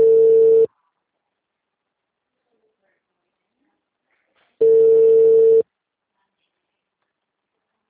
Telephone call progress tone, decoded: ringback tone

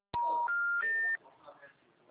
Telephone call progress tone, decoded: special information tone

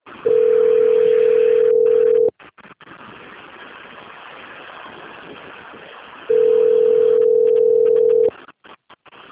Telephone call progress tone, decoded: ringback tone

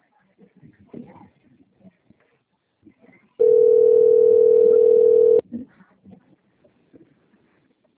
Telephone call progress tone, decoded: ringback tone

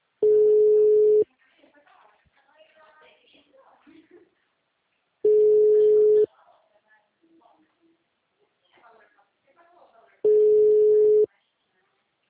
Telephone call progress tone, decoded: ringback tone

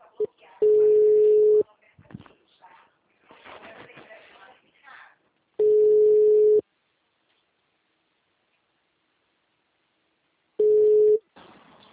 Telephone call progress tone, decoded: ringback tone

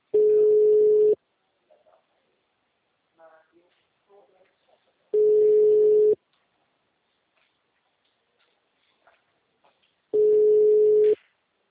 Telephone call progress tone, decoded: ringback tone

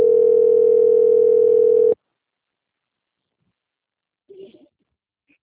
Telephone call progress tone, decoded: ringback tone

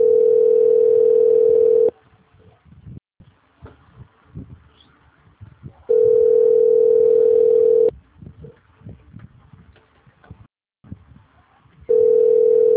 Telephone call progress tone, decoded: ringback tone

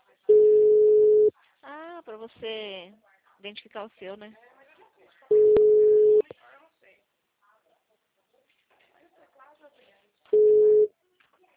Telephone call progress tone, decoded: ringback tone

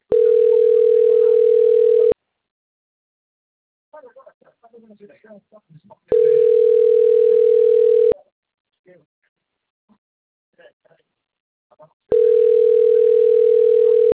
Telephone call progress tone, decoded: ringback tone

